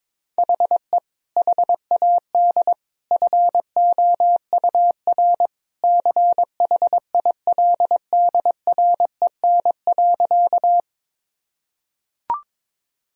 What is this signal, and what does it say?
Morse code, heard as a keyed tone, about 21 wpm, 700 Hz. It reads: HE HAD FOUR CHILDREN.
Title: Sentences from Top 1000 Words - Review Part 1 - 22wpm